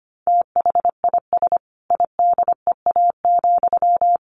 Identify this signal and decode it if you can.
Morse code, heard as a keyed tone, about 25 wpm, 700 Hz. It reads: THIS IDEA,
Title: Sentences from Top 1300 Words 25wpm